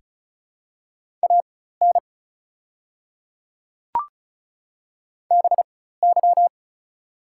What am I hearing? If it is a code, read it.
Morse code, heard as a keyed tone, about 36 wpm, spaced as for 16 wpm, 690 Hz. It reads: AN BY
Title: 4x Character Spacing - Top 300 Words - 35wpm